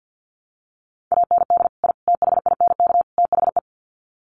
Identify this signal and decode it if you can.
Morse code, heard as a keyed tone, about 50 wpm, 700 Hz. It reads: AND I THINK THE
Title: Sets of 4 Words 50wpm